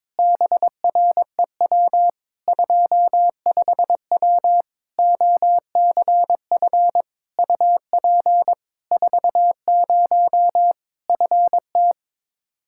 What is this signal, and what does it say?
Morse code, heard as a keyed tone, about 21 wpm, 700 Hz. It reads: BREW 25W OCF UP 40 FT
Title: Practice QSO - Exchange 2 - Info - 22wpm